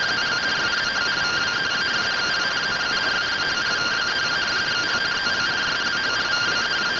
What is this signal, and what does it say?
Two-tone teleprinter signal, about 170 Hz shift.